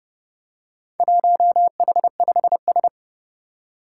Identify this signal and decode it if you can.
Morse code, heard as a keyed tone, about 30 wpm, 700 Hz. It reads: 1H5S
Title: Four Challenging Letter-Number Combinations 30wpm